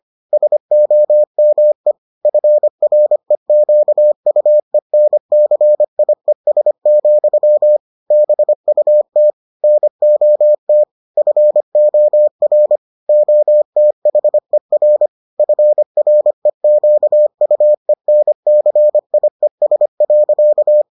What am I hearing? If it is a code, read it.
Morse code, heard as a keyed tone, about 25 wpm, 600 Hz. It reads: SOME FREQUENCIES, BUT NOT FOR OTHER FREQUENCIES.